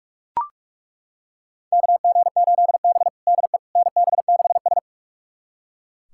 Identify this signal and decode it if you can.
Morse code, heard as a keyed tone, about 42 wpm, 700 Hz. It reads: KG8B DE NB6S
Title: USA Callsign DE Callsign - 4 Characters - 45wpm